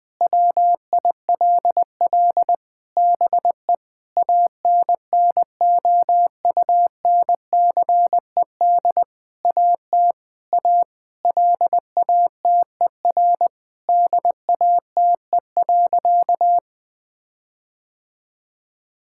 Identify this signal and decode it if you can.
Morse code, heard as a keyed tone, about 20 wpm, 700 Hz. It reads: WILL BE ANNOUNCED AT A LATER DATE.
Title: Sentences with 7-Word Phrases - 20wpm